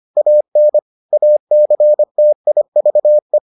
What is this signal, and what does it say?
Morse code, heard as a keyed tone, about 25 wpm, 600 Hz. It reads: AN ACTIVE